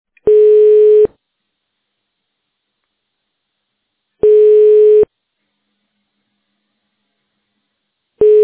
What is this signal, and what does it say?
Ringback tone (425 Hz, 1 s on, 3 s off).
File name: 1040.3000.e48dfbcb-8e82-4fe7-b918